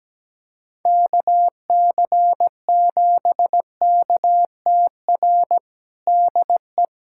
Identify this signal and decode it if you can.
Morse code, heard as a keyed tone, about 17 wpm, 700 Hz. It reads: KC7KTR DE
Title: USA Callsign DE Callsign 17wpm